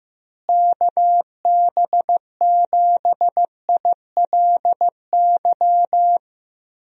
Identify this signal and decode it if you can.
Morse code, heard as a keyed tone, about 15 wpm, 700 Hz. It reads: KB7ILY
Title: USA Call Signs - 6 Characters - Repeated 3x - 15wpm